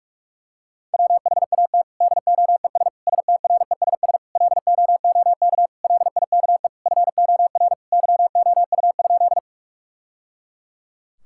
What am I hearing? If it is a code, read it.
Morse code, heard as a keyed tone, about 47 wpm, 700 Hz. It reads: WHAT DOES STRESS LOOK LIKE FOR YOU?
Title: Sentences from Top 1000 Words - Review Part 6 - 45wpm